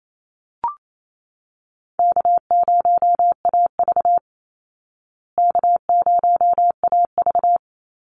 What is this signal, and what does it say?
Morse code, heard as a keyed tone, about 28 wpm, 700 Hz. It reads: K0AV K0AV
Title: Easy Call Signs Repeated 2x - 28wpm